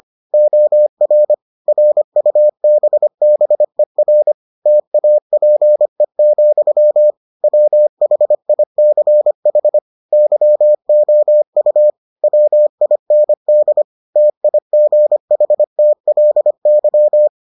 Morse code, heard as a keyed tone, about 25 wpm, 600 Hz. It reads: OR RUBBER TAPE, WHICH YOU WIND TIGHTLY